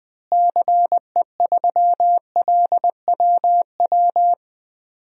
Morse code, heard as a keyed tone, about 21 wpm, 700 Hz. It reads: CE3LWW
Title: DXCC-100 Call Signs - 6 Characters - Repeated 3x - 20wpm